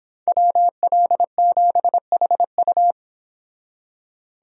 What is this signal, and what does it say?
Morse code, heard as a keyed tone, about 26 wpm, 700 Hz. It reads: WL7HU